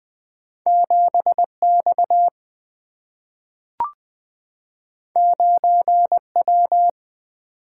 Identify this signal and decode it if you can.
Morse code, heard as a keyed tone, about 20 wpm, 700 Hz. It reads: 7 X 9 W
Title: Single-Letter Single-Number 20wpm